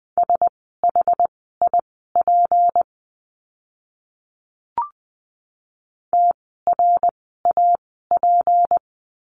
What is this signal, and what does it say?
Morse code, heard as a keyed tone, about 20 wpm, spaced as for 15 wpm, 700 Hz. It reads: SHIP TRAP
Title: 153 Lesson 25f - Letter p - Words 20wpm